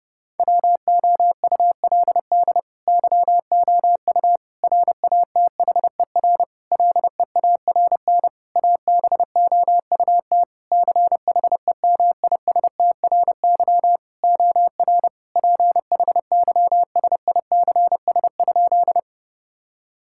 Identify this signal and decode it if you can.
Morse code, heard as a keyed tone, about 30 wpm, 700 Hz. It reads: WOULD YOU RATHER LEARN ABOUT CHEMISTRY OR PHYSICS?
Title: Sentences from Top 1000 Words - Review Part 6 - 30wpm